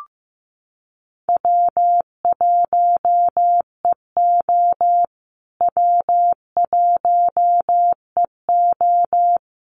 Morse code, heard as a keyed tone, about 15 wpm, 700 Hz. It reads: W1EO W1EO
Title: USA Call Signs Repeated 3x 15wpm